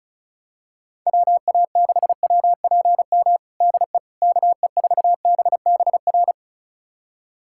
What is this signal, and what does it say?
Morse code, heard as a keyed tone, about 36 wpm, 700 Hz. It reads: WA6WPM DE KE4BBR
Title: USA Callsign DE Callsign - 6 Characters - 35wpm